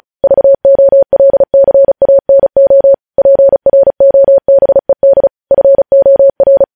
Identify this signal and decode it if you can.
Morse code, heard as a keyed tone, about 36 wpm, 550 Hz. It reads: VOLCANO PROBED FOR
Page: news headlines in morse code at 35 wpm